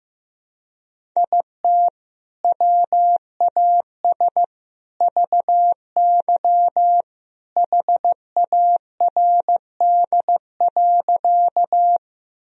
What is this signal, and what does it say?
Morse code, heard as a keyed tone, about 15 wpm, 700 Hz. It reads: IT WAS VY HARD.
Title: Sentences from Top 300 Words and Abbreviations 15wpm